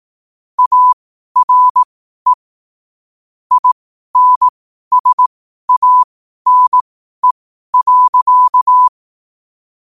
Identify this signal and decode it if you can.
Morse code, heard as a keyed tone, about 18 wpm, spaced as for 12 wpm, 1000 Hz. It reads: ARE INSANE.
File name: qotdfiles / 12wpm